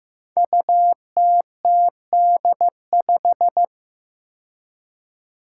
Morse code, heard as a keyed tone, about 15 wpm, 700 Hz. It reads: UTTD5
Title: Five Challenging Letter-Number Combinations 15wpm